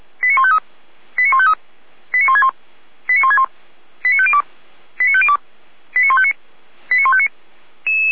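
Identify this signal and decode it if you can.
Five-tone selective call (ZVEI), heard as quick runs of five tones, tones of about 70 ms. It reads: 89153 89153 89161 89161 89502 89502 89159 89159
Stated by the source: ZVEI five-tone selective call received via SDR.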